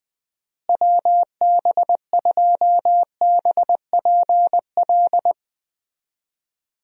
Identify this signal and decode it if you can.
Morse code, heard as a keyed tone, about 20 wpm, 700 Hz. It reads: WB2BPL
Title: USA Call Signs - 6 Characters - Repeated 3x - 20wpm